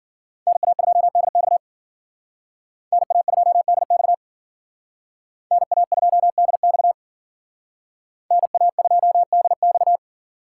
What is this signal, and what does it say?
Morse code, heard as a keyed tone, about 50 wpm, 700 Hz. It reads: NA2DX NA2DX NA2DX NA2DX
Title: Speed Racing - USA Call Signs 40wpm